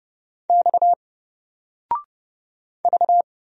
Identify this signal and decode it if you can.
Morse code, heard as a keyed tone, about 30 wpm, 700 Hz. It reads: X V
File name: Single-Letter-Number-30wpm